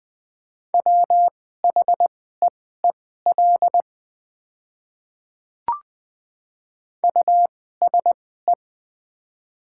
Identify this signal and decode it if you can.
Morse code, heard as a keyed tone, about 20 wpm, spaced as for 14 wpm, 700 Hz. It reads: WHEEL USE
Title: 2x Character Spacing - Top 500 Words - 20wpm